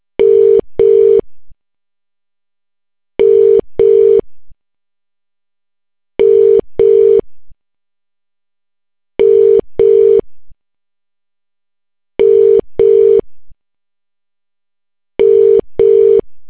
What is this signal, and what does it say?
Ringback tone (UK), 400 + 450 Hz.